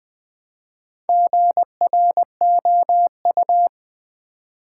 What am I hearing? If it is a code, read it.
Morse code, heard as a keyed tone, about 20 wpm, 690 Hz. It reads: GROU
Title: Build A Word - 2 Letters At A Time - 6-Letter Words - 20wpm